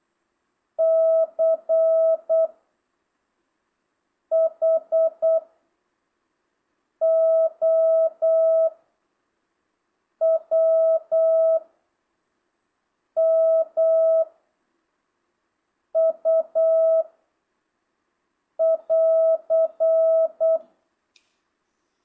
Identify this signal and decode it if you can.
Morse code, heard as a keyed tone, about 8 wpm, 650 Hz. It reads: C H O W M U <AR>
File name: jh8cbh / 2025cw